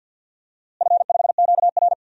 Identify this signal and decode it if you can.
Morse code, heard as a keyed tone, about 51 wpm, 700 Hz. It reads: F5QR
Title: DXCC-100 Call Signs - 4 Characters - Repeated 2x - 50wpm